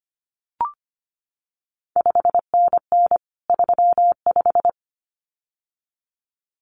Morse code, heard as a keyed tone, about 25 wpm, 700 Hz. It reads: 5NN 35
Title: CQ WW DX Contest Exchange Encore - 25wpm